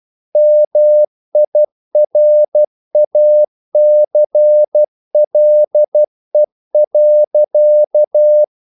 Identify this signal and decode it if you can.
Morse code, heard as a keyed tone, about 12 wpm, 600 Hz. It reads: MIRACLE.
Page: English quotations in CW (telegraphy - morse)